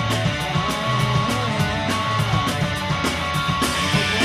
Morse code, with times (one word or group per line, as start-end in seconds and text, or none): none